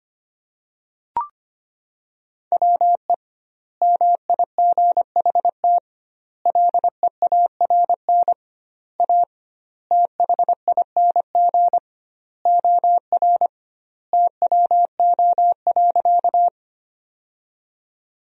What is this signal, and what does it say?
Morse code, heard as a keyed tone, about 25 wpm, 700 Hz. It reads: WE MIGHT LEARN A THING OR TWO.
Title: 2x Word-Spacing - Sentences from Top 300 Words - 25wpm